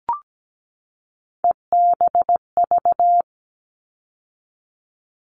Morse code, heard as a keyed tone, about 17 wpm, 700 Hz. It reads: EBV
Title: All Cut Numbers - 3-Digit 17wpm